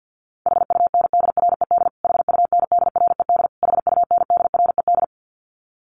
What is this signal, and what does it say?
Morse code, heard as a keyed tone, about 50 wpm, 700 Hz. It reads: HUNDRED HUNDRED HUNDRED
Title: Top 500 Words Repeated 50wpm